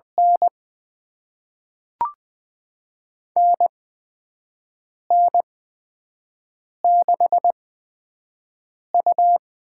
Morse code, heard as a keyed tone, about 20 wpm, spaced as for 5 wpm, 700 Hz. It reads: N NN6U
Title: USA Call Signs - 4 Characters - 8x Character Spacing - 20wpm